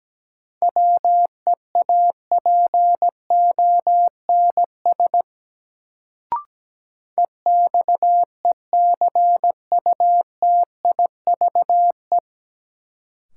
Morse code, heard as a keyed tone, about 17 wpm, 700 Hz. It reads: WEAPONS EXECUTIVE
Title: Sets of 5 Words Encore - New Word Drill - 17wpm